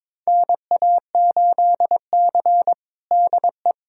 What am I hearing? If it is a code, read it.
Morse code, heard as a keyed tone, about 22 wpm, 700 Hz. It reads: NA8C DE
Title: USA Callsign DE Callsign - 4 Characters - 22wpm